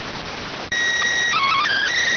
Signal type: SSTV (slow-scan television)

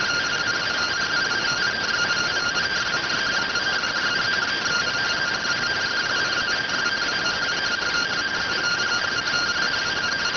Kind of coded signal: two-tone teleprinter signal

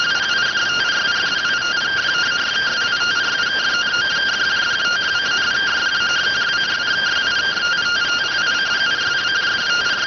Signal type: two-tone teleprinter signal

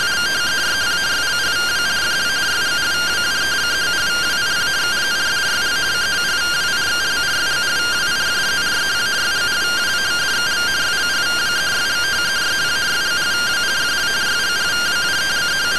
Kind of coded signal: two-tone teleprinter signal

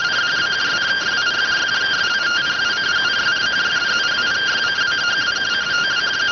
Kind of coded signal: two-tone teleprinter signal

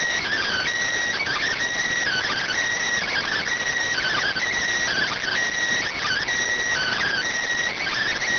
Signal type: Emergency Alert System (SAME) tones

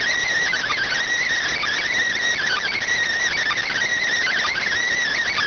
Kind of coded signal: Emergency Alert System (SAME) tones

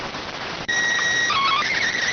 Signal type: SSTV (slow-scan television)

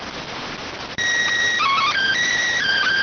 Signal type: SSTV (slow-scan television)

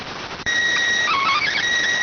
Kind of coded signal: SSTV (slow-scan television)

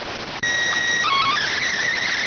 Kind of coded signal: SSTV (slow-scan television)

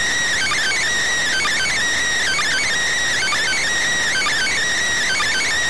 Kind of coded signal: Emergency Alert System (SAME) tones

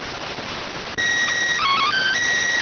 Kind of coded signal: SSTV (slow-scan television)

